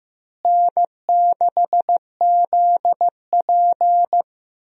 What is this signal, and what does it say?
Morse code, heard as a keyed tone, about 15 wpm, 700 Hz. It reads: N6ZP
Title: USA Call Signs - 4 Characters - Repeated 3x - 15wpm